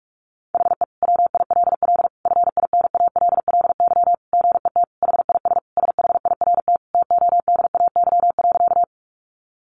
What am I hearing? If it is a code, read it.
Morse code, heard as a keyed tone, about 44 wpm, 700 Hz. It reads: HE WILL FINALLY GET HIS SHIRT TODAY.
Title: Additional Easy Phrases 45wpm